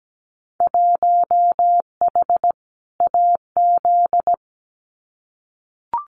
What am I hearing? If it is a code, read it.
Morse code, heard as a keyed tone, about 17 wpm, 700 Hz. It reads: 1H AZ
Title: Winter Field Day Exchanges 17wpm